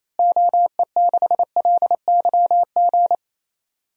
Morse code, heard as a keyed tone, about 28 wpm, 700 Hz. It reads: OE6LYG
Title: DXCC-100 Call Signs - 6 Characters - Repeated 2x - 28wpm